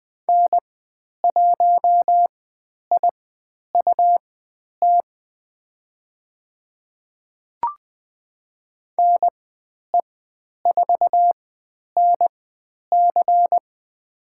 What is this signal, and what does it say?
Morse code, heard as a keyed tone, about 20 wpm, spaced as for 10 wpm, 700 Hz. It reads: N1IUT NE4NC
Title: 109 Lesson 18g - Letter m - Call Signs 20wpm